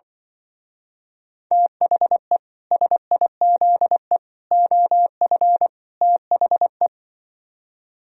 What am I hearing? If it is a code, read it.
Morse code, heard as a keyed tone, about 23 wpm, 700 Hz. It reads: THE SIZE OF THE
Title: Speed-Racing - Sets of 4 Words 20wpm